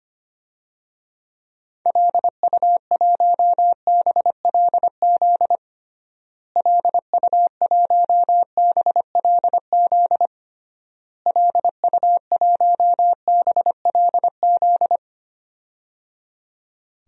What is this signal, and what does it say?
Morse code, heard as a keyed tone, about 25 wpm, 700 Hz. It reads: LU1BLZ LU1BLZ LU1BLZ
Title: DXCC-100 Call Signs - 6 Characters - Repeated 3x - 25wpm